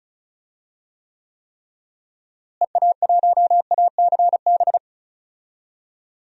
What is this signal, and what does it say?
Morse code, heard as a keyed tone, about 36 wpm, 700 Hz. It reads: EA1ACB
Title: DXCC-100 Call Signs - 6 Characters - Repeated 3x - 35wpm